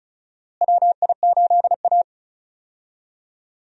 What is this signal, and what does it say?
Morse code, heard as a keyed tone, about 34 wpm, 700 Hz. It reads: WI8A
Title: USA Call Signs - 4 Characters - Repeated 3x - 35wpm